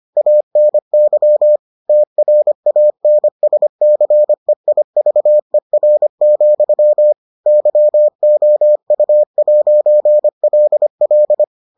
Morse code, heard as a keyed tone, about 25 wpm, 600 Hz. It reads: ANY TRANSCEIVER, YOU'LL